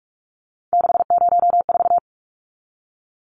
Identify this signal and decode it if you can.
Morse code, heard as a keyed tone, about 44 wpm, 700 Hz. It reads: 604
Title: Numbers - 3-Digit 45wpm